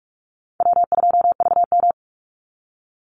Morse code, heard as a keyed tone, about 44 wpm, 700 Hz. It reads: W2VM
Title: Call Signs Repeated 2x - 45wpm